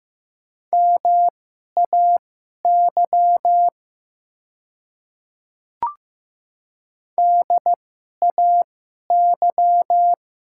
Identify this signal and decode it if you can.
Morse code, heard as a keyed tone, about 15 wpm, spaced as for 11 wpm, 700 Hz. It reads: MAY DAY
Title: 2x Character Spacing - Top 100 Words - 15wpm